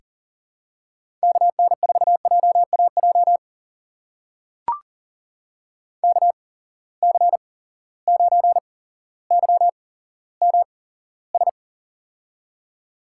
Morse code, heard as a keyed tone, about 41 wpm, spaced as for 17 wpm, 700 Hz. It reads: KN4JAJ KC9YMS
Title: USA Call Signs - 6 Characters - 8x Character Spacing - 40wpm